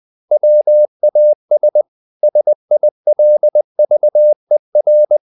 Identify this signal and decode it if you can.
Morse code, heard as a keyed tone, about 20 wpm, 600 Hz. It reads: WAS SILVER
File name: the old man and the sea part 2 20wpm